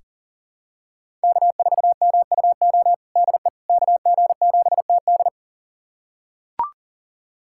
Morse code, heard as a keyed tone, about 40 wpm, 700 Hz. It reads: K4MUO DE KG7TD
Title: USA Callsign DE Callsign - 5 Characters - 40wpm